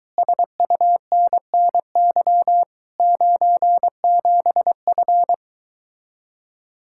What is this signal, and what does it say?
Morse code, heard as a keyed tone, about 23 wpm, 700 Hz. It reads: SUNNY 97F